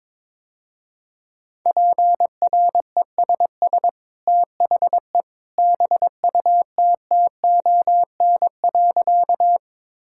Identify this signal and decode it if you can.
Morse code, heard as a keyed tone, about 22 wpm, 700 Hz. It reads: PRESS THE BUTTON.
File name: practice / 22wpm / Easy Sentences with Hints - 22wpm